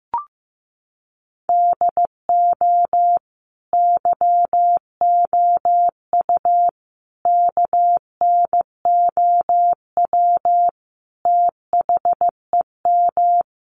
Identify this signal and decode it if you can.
Morse code, heard as a keyed tone, about 15 wpm, 700 Hz. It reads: DO YOU KNOW THEM
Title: Sentences from Top 1400 Words 15wpm